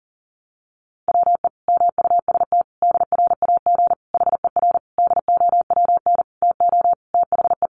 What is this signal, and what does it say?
Morse code, heard as a keyed tone, about 37 wpm, 700 Hz. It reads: WE MUST DRAG HER DOWN TO THE
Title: Sentences from Top 1500 Words 40wpm